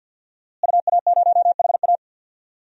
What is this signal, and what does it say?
Morse code, heard as a keyed tone, about 46 wpm, 700 Hz. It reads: UA0HA
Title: DXCC-100 Call Signs - 5 Characters - Repeated 3x - 50wpm